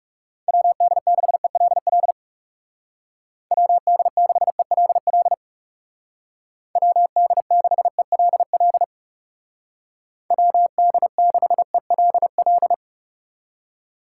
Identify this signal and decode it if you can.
Morse code, heard as a keyed tone, about 37 wpm, 700 Hz. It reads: WD6ELL WD6ELL WD6ELL WD6ELL
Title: Speed Racing - USA 2x3 Call Signs 30wpm